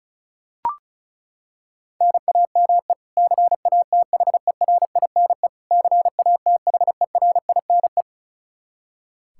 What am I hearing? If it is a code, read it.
Morse code, heard as a keyed tone, about 36 wpm, 700 Hz. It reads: NAME CATHERINE CATHERINE
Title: Practice QSO - Exchange 1 - Beginning - 35wpm